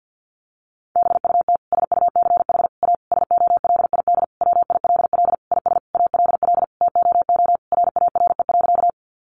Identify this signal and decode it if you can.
Morse code, heard as a keyed tone, about 50 wpm, 700 Hz. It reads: BUT SUCH A SOLID WILL IS ALL TOO RARE.
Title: Sentences from Top 1500 Words 50wpm